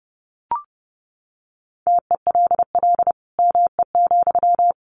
Morse code, heard as a keyed tone, about 30 wpm, 700 Hz. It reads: TELL ME,
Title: Sentences from Top 1500 Words 30wpm